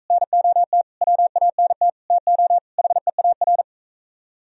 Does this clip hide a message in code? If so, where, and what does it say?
Yes: Morse code from 0.1 to 3.6 s, heard as a keyed tone, about 41 wpm, 700 Hz. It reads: NOT WANT TO HEAR